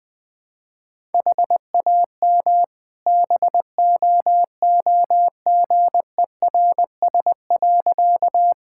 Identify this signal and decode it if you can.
Morse code, heard as a keyed tone, about 20 wpm, 700 Hz. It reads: HAM BOOGERS.
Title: All-Age Appropriate Jokes - 20wpm